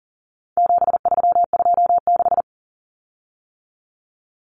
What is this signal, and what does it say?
Morse code, heard as a keyed tone, about 40 wpm, 700 Hz. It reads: 7326
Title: Common Cut Numbers - 4-Digit 40wpm